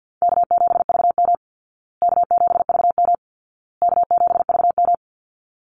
Morse code, heard as a keyed tone, about 50 wpm, 700 Hz. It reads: K7VM K7VM K7VM
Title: Call Signs Repeated 2x - 50wpm